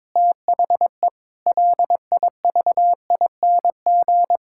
Morse code, heard as a keyed tone, about 22 wpm, 700 Hz. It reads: THE LIVING
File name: practice / 22wpm / Sentences with 4-Word Phrases - 22wpm